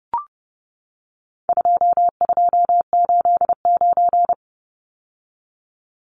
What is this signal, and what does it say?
Morse code, heard as a keyed tone, about 30 wpm, 700 Hz. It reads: 2289
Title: Numbers - 4-Digit 30wpm